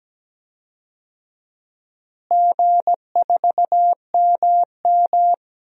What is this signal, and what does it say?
Morse code, heard as a keyed tone, about 17 wpm, 700 Hz. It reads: G4MM
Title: DXCC-100 Call Signs - 4 Characters - Repeated 2x - 17wpm